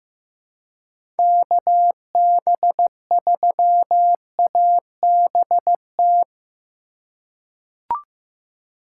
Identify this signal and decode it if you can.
Morse code, heard as a keyed tone, about 15 wpm, 700 Hz. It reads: KB3ABT
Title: USA Call Signs - 6 Characters - 6x Character Spacing - 15wpm